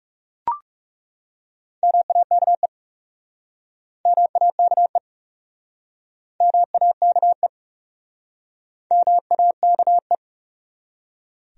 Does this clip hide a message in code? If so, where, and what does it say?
Yes: Morse code from 1.8 to 10.2 s, heard as a keyed tone, about 42 wpm, 700 Hz. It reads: MAKE MAKE MAKE MAKE